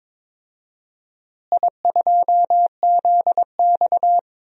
Morse code, heard as a keyed tone, about 22 wpm, 700 Hz. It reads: I2ZX
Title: DXCC-100 Call Signs - 4 Characters - Repeated 2x - 22wpm